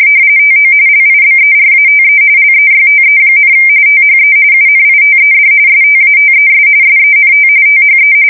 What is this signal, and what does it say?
RTTY, about 45.45 baud. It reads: TO LIBERATE LEYTE, THE UNITED STATES SENT THE
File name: fsk170 / battleshipnj